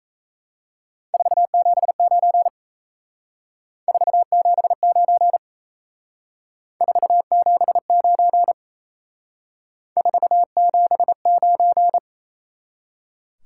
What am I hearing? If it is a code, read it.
Morse code, heard as a keyed tone, about 35 wpm, 700 Hz. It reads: 479 479 479 479